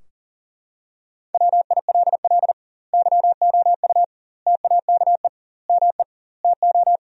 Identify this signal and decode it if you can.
Morse code, heard as a keyed tone, about 40 wpm, 700 Hz. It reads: WILL YOU TAKE ME TO
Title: 2x Word-Spacing - Sentences from Top 200 Words - 40wpm